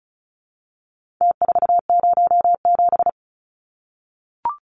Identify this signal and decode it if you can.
Morse code, heard as a keyed tone, about 34 wpm, 700 Hz. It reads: T407
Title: Numbers - Contest Style - 4-Digit 35wpm